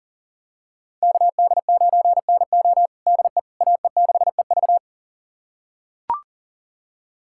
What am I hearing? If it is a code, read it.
Morse code, heard as a keyed tone, about 40 wpm, 700 Hz. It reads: KD9NO DE AE6EV
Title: USA Callsign DE Callsign - 5 Characters - 40wpm